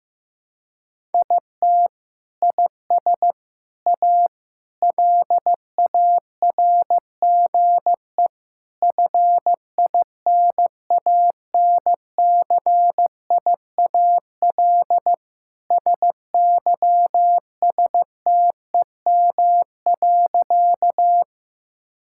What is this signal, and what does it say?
Morse code, heard as a keyed tone, about 16 wpm, 700 Hz. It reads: IT IS A LARGE FINANCIAL SYSTEM.
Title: Sentences from Top 1000 Words - Review Part 5 - 15wpm